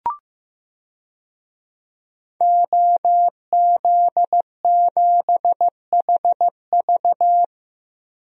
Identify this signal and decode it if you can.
Morse code, heard as a keyed tone, about 15 wpm, 700 Hz. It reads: OZ7HV
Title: DXCC-100 Call Signs - 5 Characters - 15wpm